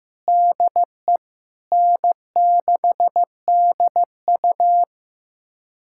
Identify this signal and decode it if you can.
Morse code, heard as a keyed tone, about 15 wpm, 700 Hz. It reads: DE N6DU
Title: USA Callsign DE Callsign - 4 Characters - 15wpm